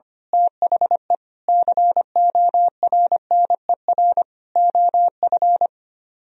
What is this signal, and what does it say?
Morse code, heard as a keyed tone, about 24 wpm, 700 Hz. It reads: THE CORNER OF
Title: Speed-Racing - Sets of 5 Words 25wpm